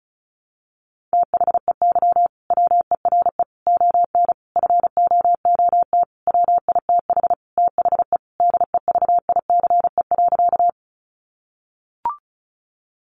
Morse code, heard as a keyed tone, about 34 wpm, 700 Hz. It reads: THEY WERE ON FOOT WITH THE DEVICE.
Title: Sentences from Top 1500 Words 35wpm